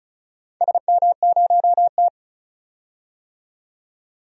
Morse code, heard as a keyed tone, about 36 wpm, 700 Hz. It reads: SM0T
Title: DXCC-100 Call Signs - 4 Characters - 35wpm